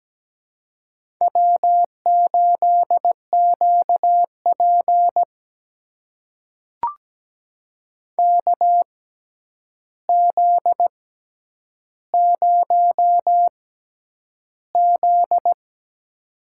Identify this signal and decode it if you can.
Morse code, heard as a keyed tone, about 17 wpm, spaced as for 9 wpm, 700 Hz. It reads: W8QP KZ0Z
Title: USA Call Signs - 4 Characters - 6x Character Spacing - 17wpm